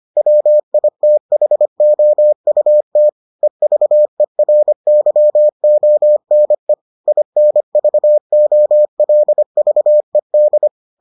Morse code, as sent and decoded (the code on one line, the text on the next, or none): .-- .. - .... --- ..- - / . ...- . .-. -.-- --- -. / .. -. ...- --- .-.. ...- . -..
WITHOUT EVERYONE INVOLVED